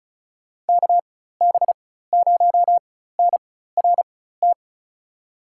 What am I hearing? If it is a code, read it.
Morse code, heard as a keyed tone, about 34 wpm, spaced as for 16 wpm, 700 Hz. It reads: KB0NRT
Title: USA Call Signs - 6 Characters - 4x Character Spacing - 35wpm